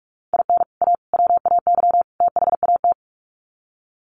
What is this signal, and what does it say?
Morse code, heard as a keyed tone, about 44 wpm, 700 Hz. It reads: IN A WAY THAT
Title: Sets of 4 Words 45wpm